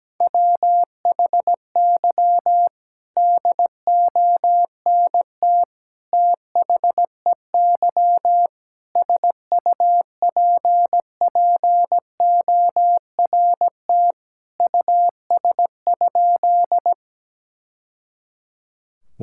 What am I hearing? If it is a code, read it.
Morse code, heard as a keyed tone, about 17 wpm, 700 Hz. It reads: WHY DONT THEY SUPPORT US?
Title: Sentences from Top 1000 Words - Review Part 5 - 17wpm